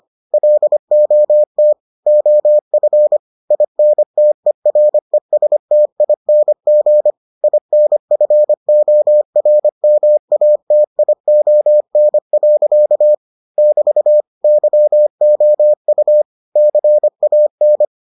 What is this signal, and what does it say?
Morse code, heard as a keyed tone, about 25 wpm, 600 Hz. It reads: LOT OF INTERESTING INFORMATION. <BT> YOU CAN